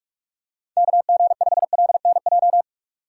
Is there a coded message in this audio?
Yes: Morse code from 0.8 to 2.6 s, heard as a keyed tone, about 45 wpm, 700 Hz. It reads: KG5LNJ